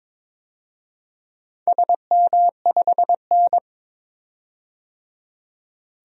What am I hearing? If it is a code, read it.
Morse code, heard as a keyed tone, about 22 wpm, 700 Hz. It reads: SM5N
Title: DXCC-100 Call Signs - 4 Characters - 22wpm